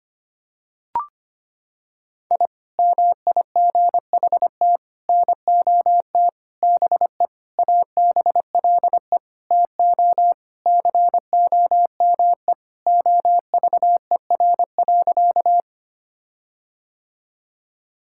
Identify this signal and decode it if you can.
Morse code, heard as a keyed tone, about 25 wpm, 700 Hz. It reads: I MIGHT NOT BE ABLE TO COME OVER.
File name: practice / 25wpm / Sentences with 4-Word Phrases - 25wpm